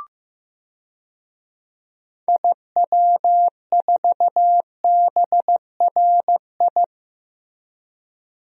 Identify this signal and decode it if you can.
Morse code, heard as a keyed tone, about 15 wpm, 700 Hz. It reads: IW4BRI
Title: DXCC-100 Call Signs - 6 Characters - 15wpm